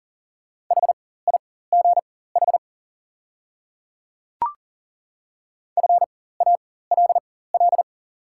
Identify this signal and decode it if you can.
Morse code, heard as a keyed tone, about 40 wpm, spaced as for 19 wpm, 690 Hz. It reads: HIGH FALL
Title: 4x Character Spacing - Top 500 Words - 40wpm